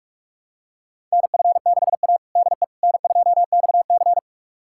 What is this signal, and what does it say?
Morse code, heard as a keyed tone, about 42 wpm, 700 Hz. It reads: NF6A DE N2XC
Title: USA Callsign DE Callsign - 4 Characters - 45wpm